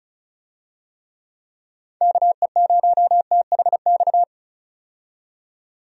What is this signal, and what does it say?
Morse code, heard as a keyed tone, about 36 wpm, 700 Hz. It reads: KE0THX
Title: USA Call Signs - 6 Characters - 35wpm